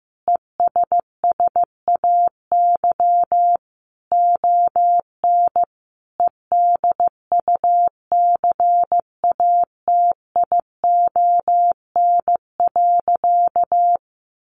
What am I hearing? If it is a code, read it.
Morse code, heard as a keyed tone, about 15 wpm, 700 Hz. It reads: ESSAY ON EDUCATION.
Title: Sentences from Top 1500 Words 15wpm